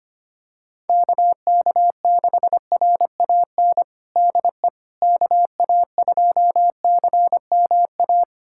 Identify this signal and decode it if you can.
Morse code, heard as a keyed tone, about 25 wpm, 700 Hz. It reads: KK6RAN DE KA2CMA
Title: USA Callsign DE Callsign - 6 Characters - 25wpm